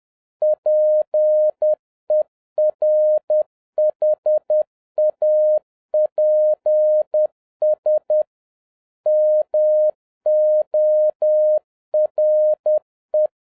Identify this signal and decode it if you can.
Morse code, heard as a keyed tone, about 10 wpm, 610 Hz. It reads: PERHAPS MORE
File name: 2011-07-mp3 / n21 GREETINGS FROM SINGAPORE 10wpm